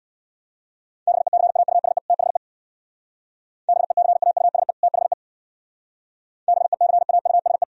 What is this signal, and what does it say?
Morse code, heard as a keyed tone, about 70 wpm, 700 Hz. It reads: BECAUSE THE BECAUSE THE BECAUSE